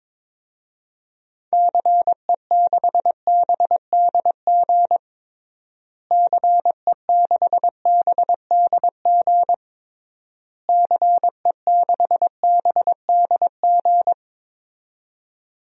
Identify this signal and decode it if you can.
Morse code, heard as a keyed tone, about 22 wpm, 700 Hz. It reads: CE6BDG CE6BDG CE6BDG
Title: DXCC-100 Call Signs - 6 Characters - Repeated 3x - 22wpm